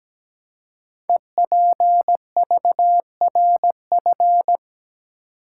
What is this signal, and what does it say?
Morse code, heard as a keyed tone, about 17 wpm, 700 Hz. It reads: EPVRF
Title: Code Groups - 5 Letters - 17wpm